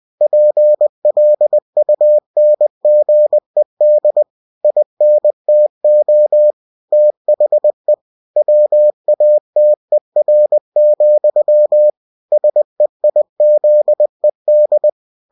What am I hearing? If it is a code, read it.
Morse code, heard as a keyed tone, about 20 wpm, 600 Hz. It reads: PLUNGED INTO THE WATER, SEIZED
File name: cw-books / winnie-the-pooh / winnie-20wpm-page-0094